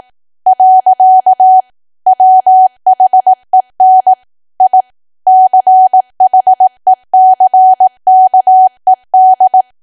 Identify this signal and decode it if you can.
Morse code, heard as a keyed tone, about 18 wpm, 750 Hz. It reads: . WHEN I CHECKED
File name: cwa / W1AW-Level2 / 18WPM